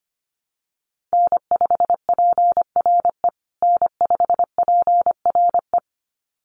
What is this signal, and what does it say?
Morse code, heard as a keyed tone, about 25 wpm, 700 Hz. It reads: N5PRE N5PRE
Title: USA Call Signs Repeated 2x 25wpm